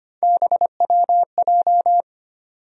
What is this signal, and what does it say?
Morse code, heard as a keyed tone, about 25 wpm, 700 Hz. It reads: BWJ
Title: USA Call Sign Postfixes - 3 Letters 25wpm